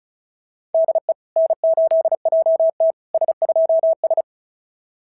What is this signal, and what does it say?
Morse code, heard as a keyed tone, about 35 wpm, 650 Hz. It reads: DE N8JT S2S
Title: SOTA QSO Practice 35wpm